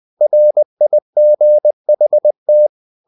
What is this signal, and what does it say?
Morse code, heard as a keyed tone, about 20 wpm, 600 Hz. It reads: RIGHT
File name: CW / Practice / EnglishQuotations0320